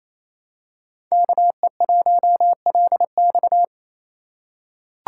Morse code, heard as a keyed tone, about 28 wpm, 700 Hz. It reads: KE1LX
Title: USA Call Signs - 5 Characters - 4x Character Spacing - 28wpm